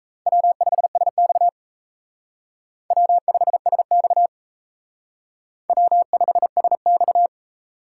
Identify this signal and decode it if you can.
Morse code, heard as a keyed tone, about 36 wpm, 700 Hz. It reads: W5SX W5SX W5SX